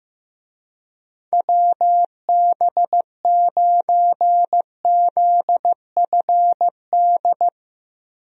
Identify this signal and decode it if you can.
Morse code, heard as a keyed tone, about 15 wpm, 700 Hz. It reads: WB9ZFD
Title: USA Call Signs - 6 Characters - Repeated 2x - 15wpm